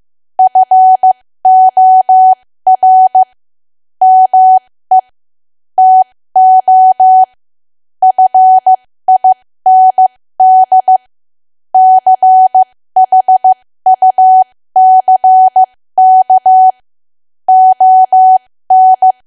Morse code, heard as a keyed tone, about 15 wpm, spaced as for 13 wpm, 750 Hz. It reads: FOR ME TO FIND CHUCK ON
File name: cwa / W1AW-Level2 / 13WPM